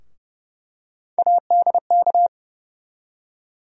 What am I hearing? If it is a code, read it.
Morse code, heard as a keyed tone, about 32 wpm, 700 Hz. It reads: ADK
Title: USA Call Sign Postfixes - 3 Letters 30wpm